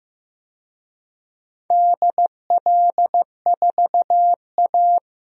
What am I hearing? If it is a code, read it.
Morse code, heard as a keyed tone, about 15 wpm, 700 Hz. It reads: DL4A
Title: DXCC-100 Call Signs - 4 Characters - Repeated 2x - 15wpm